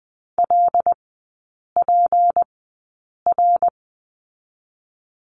Morse code, heard as a keyed tone, about 20 wpm, 700 Hz. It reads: L P R
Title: 198 Lesson 32e - Letter k - Three Characters 20wpm